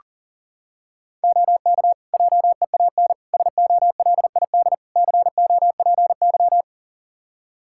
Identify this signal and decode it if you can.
Morse code, heard as a keyed tone, about 40 wpm, 700 Hz. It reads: OK JEAN SOLID COPY
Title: Practice QSO - Exchange 3 - More Info - 40wpm